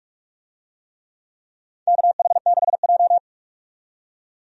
Morse code, heard as a keyed tone, about 46 wpm, 700 Hz. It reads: KH6J
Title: DXCC-100 Call Signs - 4 Characters - 45wpm